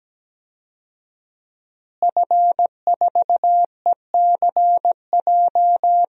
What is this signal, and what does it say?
Morse code, heard as a keyed tone, about 17 wpm, 700 Hz. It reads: F4ECJ
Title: DXCC-100 Call Signs - 5 Characters - Repeated 2x - 17wpm